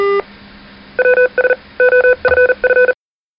Morse code, heard as a keyed tone, about 40 wpm, 500 Hz. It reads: WS OFF
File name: rpt-ws-off